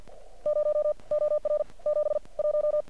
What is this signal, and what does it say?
Morse code, heard as a keyed tone, about 50 wpm, 600 Hz. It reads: 0 OR 7 1
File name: archived / CRSnet